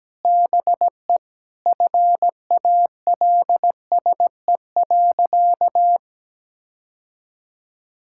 Morse code, heard as a keyed tone, about 17 wpm, 700 Hz. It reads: BE FALSE.
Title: Sentences from Top 1600 Words 17wpm